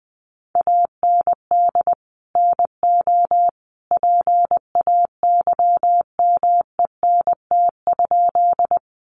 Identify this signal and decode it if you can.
Morse code, heard as a keyed tone, about 20 wpm, 700 Hz. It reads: AND NO PAYMENT?
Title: Sentences from Top 1200 Words 20wpm